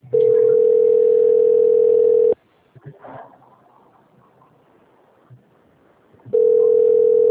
Ringback tone (North America), 440 + 480 Hz.